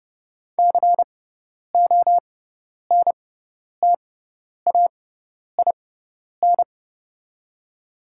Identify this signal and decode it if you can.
Morse code, heard as a keyed tone, about 31 wpm, spaced as for 10 wpm, 700 Hz. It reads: CONTAIN
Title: 6x Character Spacing - Top 500 Words - 30wpm